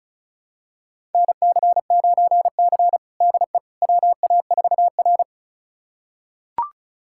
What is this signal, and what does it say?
Morse code, heard as a keyed tone, about 36 wpm, 700 Hz. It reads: NC9C DE WA4R
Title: USA Callsign DE Callsign - 4 Characters - 35wpm